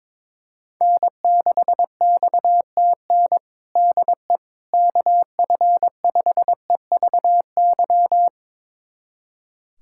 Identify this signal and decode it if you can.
Morse code, heard as a keyed tone, about 21 wpm, 700 Hz. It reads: N6XTN DE KF5EVY
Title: USA Callsign DE Callsign 22wpm